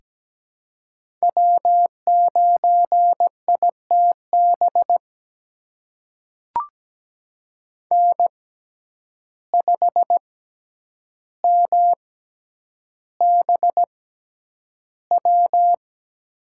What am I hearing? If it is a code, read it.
Morse code, heard as a keyed tone, about 17 wpm, spaced as for 9 wpm, 700 Hz. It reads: W9ITB N5MBW
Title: USA Call Signs - 5 Characters - 6x Character Spacing - 17wpm